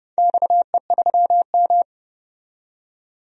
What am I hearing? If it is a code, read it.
Morse code, heard as a keyed tone, about 29 wpm, 700 Hz. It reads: XE3M
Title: DXCC-100 Call Signs - 4 Characters - 30wpm